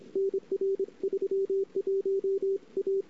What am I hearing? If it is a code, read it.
Morse code, heard as a keyed tone, about 29 wpm, 410 Hz. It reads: NR 31 A
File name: CW / 2008-ss-cw-k5tr-op-n5rz -21:00